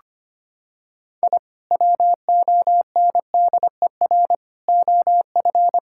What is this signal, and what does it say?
Morse code, heard as a keyed tone, about 25 wpm, 700 Hz. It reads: I WONDER OF
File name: practice / 25wpm / Sentences with 2-Word Phrases - 25wpm